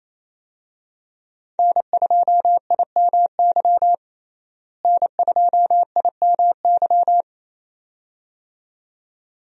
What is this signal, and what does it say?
Morse code, heard as a keyed tone, about 28 wpm, 700 Hz. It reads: N2IMY N2IMY
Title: USA Call Signs - 5 Characters - Repeated 2x - 28wpm